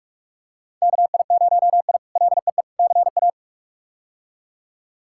Morse code, heard as a keyed tone, about 46 wpm, 700 Hz. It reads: KI0I LEE CA
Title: K1USN SST QSO Practice - 45wpm